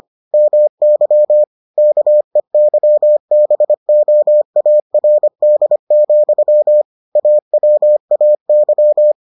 Morse code, heard as a keyed tone, about 25 wpm, 600 Hz. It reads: MY KEYBOARD, AWAY